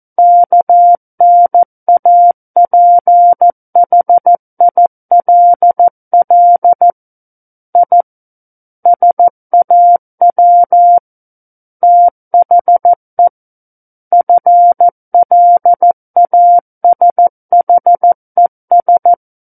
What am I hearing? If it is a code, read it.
Morse code, heard as a keyed tone, about 14 wpm, 700 Hz. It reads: KNAPHILL I SAW THE FLASHES
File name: Book One / 05-Chapter 5 - 14 WPM